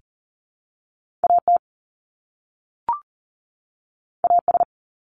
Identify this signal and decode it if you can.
Morse code, heard as a keyed tone, about 40 wpm, 700 Hz. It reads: AT AS
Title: Top 300 Words 40wpm